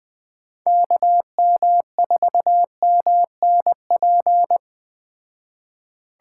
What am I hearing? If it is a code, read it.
Morse code, heard as a keyed tone, about 20 wpm, 700 Hz. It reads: KM4MNP
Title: Speed Racing - USA Call Signs 17wpm